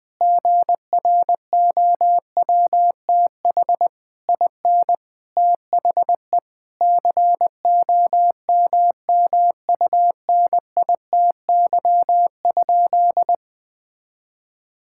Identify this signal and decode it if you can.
Morse code, heard as a keyed tone, about 20 wpm, 700 Hz. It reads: GROWTH IN THE COMMUNITY?
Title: Sentences from Top 1000 Words - Review Part 5 - 20wpm